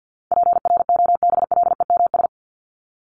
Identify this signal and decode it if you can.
Morse code, heard as a keyed tone, about 50 wpm, 700 Hz. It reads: PROBLEMS
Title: ES:S Suffix - Top 100 Words with Base Words 50wpm